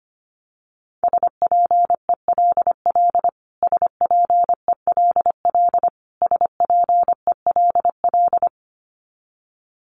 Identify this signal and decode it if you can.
Morse code, heard as a keyed tone, about 25 wpm, 700 Hz. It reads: SPELL SPELL SPELL
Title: Top 500 Words Repeated 25wpm